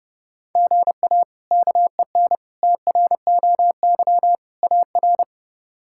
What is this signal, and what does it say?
Morse code, heard as a keyed tone, about 30 wpm, 700 Hz. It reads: GA KEN TROY AR
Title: K1USN SST QSO Practice - 30wpm